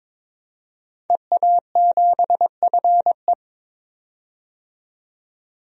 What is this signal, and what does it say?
Morse code, heard as a keyed tone, about 22 wpm, 700 Hz. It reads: EA7FE
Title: DXCC-100 Call Signs - 5 Characters - 22wpm